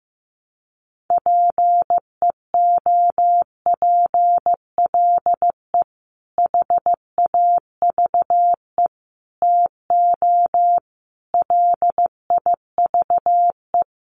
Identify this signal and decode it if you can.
Morse code, heard as a keyed tone, about 15 wpm, 700 Hz. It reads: PEOPLE HAVE TO LIVE
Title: Sentences from Top 1100 Words 15wpm